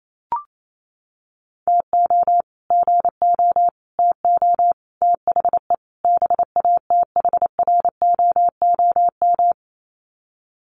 Morse code, heard as a keyed tone, about 28 wpm, 700 Hz. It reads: TO GO TO THE BATHROOM
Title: Sets of 5 Words Encore 28wpm